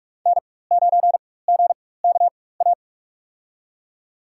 Morse code, heard as a keyed tone, about 41 wpm, spaced as for 20 wpm, 700 Hz. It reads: N9GKA
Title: USA Call Signs - 5 Characters - 4x Character Spacing - 45wpm